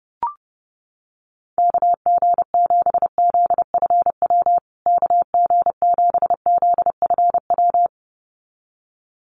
Morse code, heard as a keyed tone, about 30 wpm, 700 Hz. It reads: KG7ZFW KG7ZFW
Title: USA Call Signs Repeated 2x 30wpm